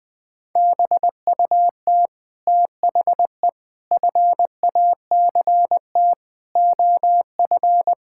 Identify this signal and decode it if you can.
Morse code, heard as a keyed tone, about 21 wpm, 700 Hz. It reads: BUT THE FACT OF
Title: Sets of 8 Words - 20wpm